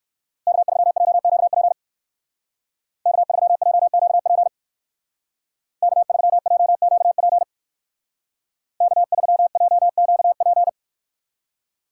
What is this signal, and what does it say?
Morse code, heard as a keyed tone, about 54 wpm, 700 Hz. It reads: K3JQP K3JQP K3JQP K3JQP
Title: Speed Racing - USA 1x3 Call Signs 45wpm